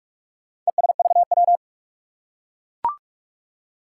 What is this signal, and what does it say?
Morse code, heard as a keyed tone, about 44 wpm, 700 Hz. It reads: ESVW
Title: Four Challenging Letter-Number Combinations 45wpm